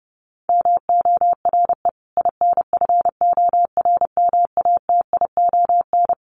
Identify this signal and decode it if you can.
Morse code, heard as a keyed tone, about 30 wpm, 700 Hz. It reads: MORE INFORMATION
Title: Sentences from Top 1200 Words 30wpm